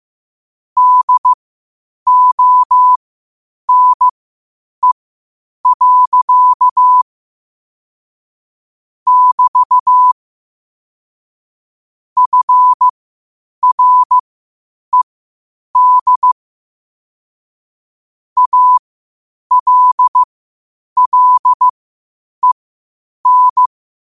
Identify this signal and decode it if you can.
Morse code, heard as a keyed tone, about 15 wpm, spaced as for 8 wpm, 1000 Hz. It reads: DONE. <BT> FRED ALLEN
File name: qotdfiles / 8wpm